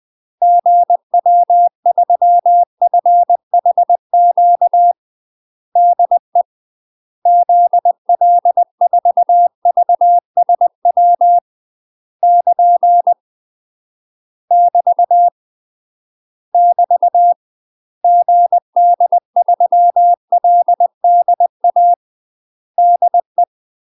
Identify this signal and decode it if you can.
Morse code, heard as a keyed tone, about 20 wpm, 700 Hz. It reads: GW3FHQ DE ZL4VSW <KN> <BT> <BT> GD3LDA DE
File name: UK MASTER QSOs TRAINING/20WPM